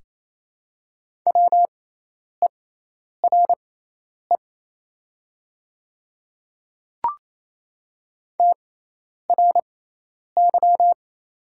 Morse code, heard as a keyed tone, about 27 wpm, spaced as for 10 wpm, 700 Hz. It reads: WERE TRY
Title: 6x Character Spacing - Top 200 Words - 28wpm